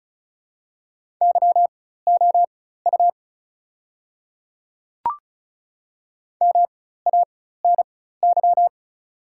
Morse code, heard as a keyed tone, about 36 wpm, spaced as for 16 wpm, 690 Hz. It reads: YOU MANY
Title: 4x Character Spacing - Top 300 Words - 35wpm